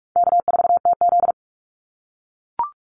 Morse code, heard as a keyed tone, about 44 wpm, 700 Hz. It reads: K4TZ
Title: Call Signs - 45wpm